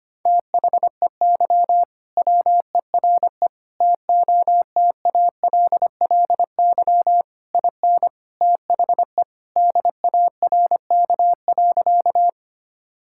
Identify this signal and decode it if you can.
Morse code, heard as a keyed tone, about 25 wpm, 700 Hz. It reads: THEY WERE TOTALLY IN THE DARK.
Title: Sentences from Top 1600 Words 25wpm